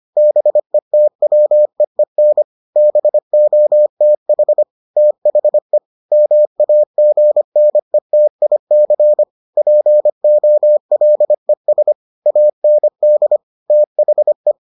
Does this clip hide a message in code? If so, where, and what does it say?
Yes: Morse code from 0.2 to 14.5 s, heard as a keyed tone, about 25 wpm, 600 Hz. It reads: BETWEEN BOTH THE MAGNETIC POLES AND THE